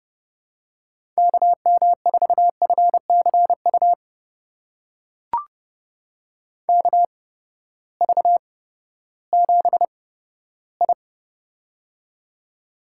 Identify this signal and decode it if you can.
Morse code, heard as a keyed tone, about 30 wpm, spaced as for 15 wpm, 700 Hz. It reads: KM4FCU KV7I
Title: USA Call Signs - 8x Character Spacing - 30wpm